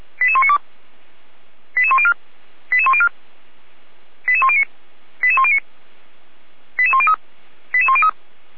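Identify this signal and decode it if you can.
Five-tone selective call (ZVEI), heard as quick runs of five tones, tones of about 70 ms. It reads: 88192 88104 88104 88109 88109 88103 88103
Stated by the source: ZVEI five-tone selective call received on SDR, fire brigade Feuerwehr Orlamünde.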